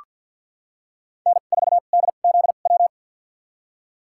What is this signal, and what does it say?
Morse code, heard as a keyed tone, about 53 wpm, spaced as for 37 wpm, 700 Hz. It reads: N4DZW
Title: USA Call Signs - 5 Characters - 2x Character Spacing - 50wpm